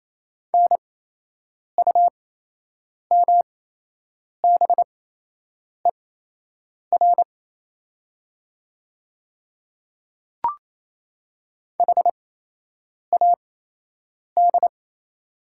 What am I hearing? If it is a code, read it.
Morse code, heard as a keyed tone, about 28 wpm, spaced as for 8 wpm, 700 Hz. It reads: NUMBER HAD
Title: 8x Character Spacing - Top 100 Words - 28wpm